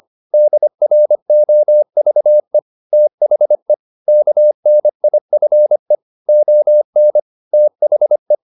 Morse code, heard as a keyed tone, about 25 wpm, 600 Hz. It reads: DROVE THE KNIFE ON THE